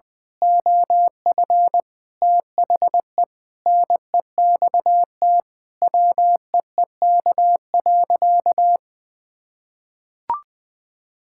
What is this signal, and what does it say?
Morse code, heard as a keyed tone, about 20 wpm, 700 Hz. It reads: OF THE NEXT WEEK.
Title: Sentences with 6-Word Phrases - 20wpm